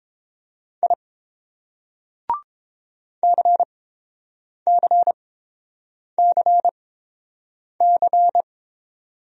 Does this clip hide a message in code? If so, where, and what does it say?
Yes: Morse code from 0.8 to 8.4 s, heard as a keyed tone, about 30 wpm, spaced as for 7 wpm, 700 Hz. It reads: I CCCC